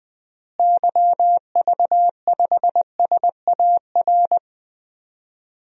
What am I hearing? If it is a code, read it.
Morse code, heard as a keyed tone, about 20 wpm, 700 Hz. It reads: YV5SAR
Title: DXCC-100 Call Signs - 6 Characters - Repeated 3x - 20wpm